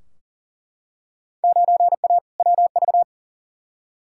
Morse code, heard as a keyed tone, about 40 wpm, 700 Hz. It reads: 9A WV
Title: Field Day Exchanges - 40wpm